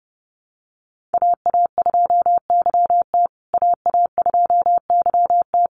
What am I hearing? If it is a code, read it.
Morse code, heard as a keyed tone, about 28 wpm, 700 Hz. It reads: AA2YT AA2YT
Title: USA Call Signs Repeated 3x 30wpm